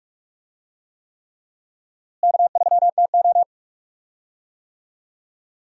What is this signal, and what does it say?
Morse code, heard as a keyed tone, about 41 wpm, 700 Hz. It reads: K3TO
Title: USA Call Signs - 4 Characters - Repeated 3x - 45wpm